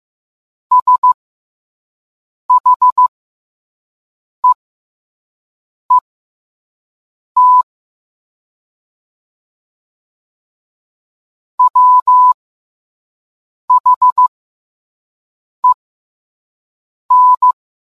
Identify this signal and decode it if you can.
Morse code, heard as a keyed tone, about 15 wpm, spaced as for 5 wpm, 1000 Hz. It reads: SHEET WHEN